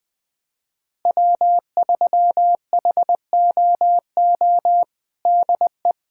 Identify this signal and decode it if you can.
Morse code, heard as a keyed tone, about 20 wpm, 700 Hz. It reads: W3HOO DE
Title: USA Callsign DE Callsign - 5 Characters - 20wpm